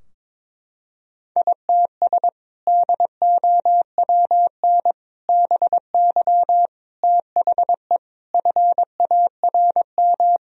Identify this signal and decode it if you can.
Morse code, heard as a keyed tone, about 22 wpm, 700 Hz. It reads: ITS DOWN BY THE FARM
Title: Sentences from Top 1000 Words - Review Part 2 - 22wpm